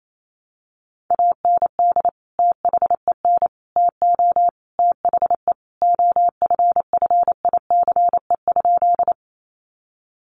Morse code, heard as a keyed tone, about 28 wpm, 700 Hz. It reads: AND THEN TO THE OFFICE?
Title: Sentences from Top 1000 Words 28wpm